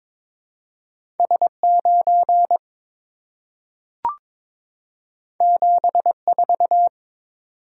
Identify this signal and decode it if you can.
Morse code, heard as a keyed tone, about 23 wpm, 690 Hz. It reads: S9 74
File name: practice / 22wpm / Two Letter-Number Combinations - 22wpm